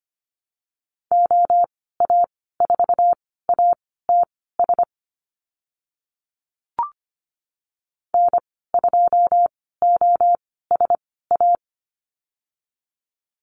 Morse code, heard as a keyed tone, about 25 wpm, spaced as for 16 wpm, 700 Hz. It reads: OA4ATS N2OSA
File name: Lesson 15g - Number 5 - Call Signs 25wpm